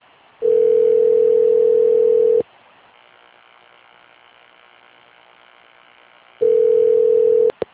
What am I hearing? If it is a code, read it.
Ringback tone (North America), 440 + 480 Hz.
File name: rg-1600-unknown- - - .6322